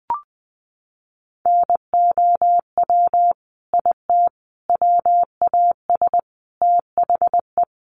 Morse code, heard as a keyed tone, about 20 wpm, 700 Hz. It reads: NOW IT WAS THE
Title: Sentences from Top 1400 Words 20wpm